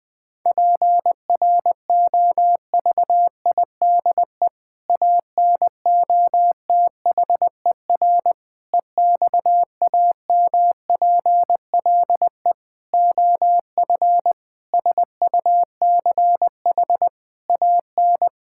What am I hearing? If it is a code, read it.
Morse code, heard as a keyed tone, about 19 wpm, 700 Hz. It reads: PROVIDE ANOTHER EXAMPLE OF SUCH AN
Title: Sentences from Top 1000 Words - Review Part 5 - 20wpm